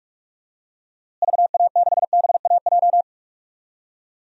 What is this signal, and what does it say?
Morse code, heard as a keyed tone, about 42 wpm, 700 Hz. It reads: VA6BAJ
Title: DXCC-100 Call Signs - 6 Characters - Repeated 3x - 45wpm